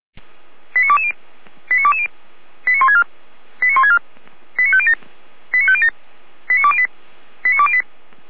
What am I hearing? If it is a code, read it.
Five-tone selective call (ZVEI), heard as quick runs of five tones, tones of about 70 ms. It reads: 89220 89220 89164 89164 89507 89507 89208 89208
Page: SDR/ZVEI